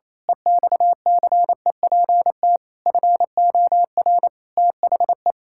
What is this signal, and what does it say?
Morse code, heard as a keyed tone, about 28 wpm, 700 Hz. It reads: EXCEPT FOR THE
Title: Pangram Challenge v2 - 28wpm